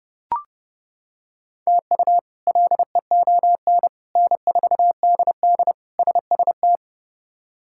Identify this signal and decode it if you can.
Morse code, heard as a keyed tone, about 31 wpm, 700 Hz. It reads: TU LEON N4DD SST
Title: K1USN SST QSO Practice - 30wpm